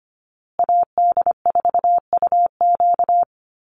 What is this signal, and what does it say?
Morse code, heard as a keyed tone, about 25 wpm, 700 Hz. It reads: AD4UQ
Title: USA Call Signs Repeated 3x 25wpm